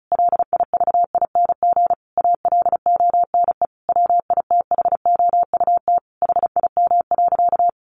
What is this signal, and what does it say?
Morse code, heard as a keyed tone, about 34 wpm, 700 Hz. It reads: LIVING ALONE WITHOUT HIM.
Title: Sentences from Top 1400 Words 35wpm